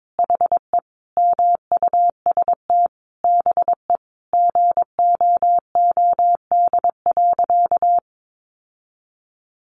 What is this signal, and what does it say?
Morse code, heard as a keyed tone, about 22 wpm, 700 Hz. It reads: HE MUST BE GOOD.
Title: Sentences from Top 1500 Words 22wpm